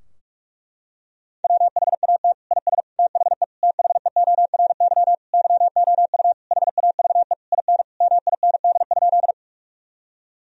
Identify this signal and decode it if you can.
Morse code, heard as a keyed tone, about 41 wpm, 700 Hz. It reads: WHAT IS THE THEORY YOU HAVE IN MIND?
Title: Sentences from Top 1000 Words - Review Part 3 - 45wpm